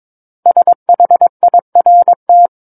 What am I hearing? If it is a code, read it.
Morse code, heard as a keyed tone, about 22 wpm, 700 Hz. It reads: SHIRT
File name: Book One / 13-Chapter 13 - 22 WPM